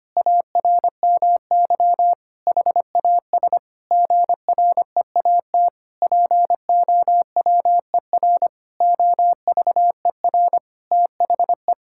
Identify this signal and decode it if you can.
Morse code, heard as a keyed tone, about 25 wpm, 700 Hz. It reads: ARMY HAS GREAT POWER OVER THE
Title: Sentences from Top 1000 Words - Review Part 5 - 25wpm